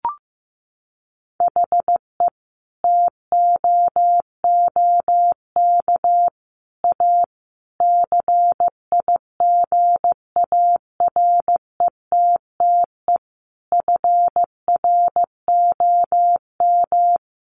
Morse code, heard as a keyed tone, about 15 wpm, 700 Hz. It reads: HE TOOK A CIGARETTE FROM
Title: Sentences from Top 1000 Words 15wpm